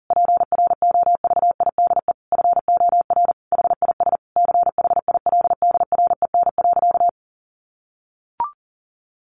Morse code, heard as a keyed tone, about 40 wpm, 700 Hz. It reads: PROVIDE FOR HIS CHILDREN.
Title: Sentences from Top 1100 Words 40wpm